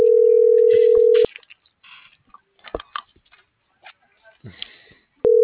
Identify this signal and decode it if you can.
Ringback tone (North America), 440 + 480 Hz.